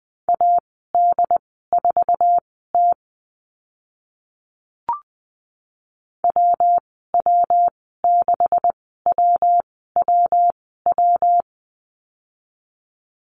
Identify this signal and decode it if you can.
Morse code, heard as a keyed tone, about 20 wpm, spaced as for 14 wpm, 700 Hz. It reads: AD4T WW6WWW
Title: 140 Lesson 23g - Letter f - Call Signs 20wpm